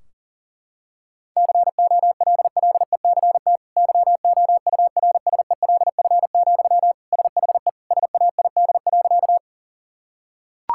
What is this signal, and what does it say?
Morse code, heard as a keyed tone, about 40 wpm, 700 Hz. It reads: COLLECT YOURSELF, SHE SAID.
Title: Sentences from Top 1600 Words 40wpm